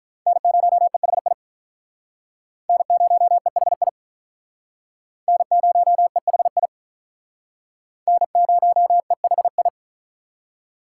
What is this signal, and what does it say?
Morse code, heard as a keyed tone, about 44 wpm, 700 Hz. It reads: N0EHI N0EHI N0EHI N0EHI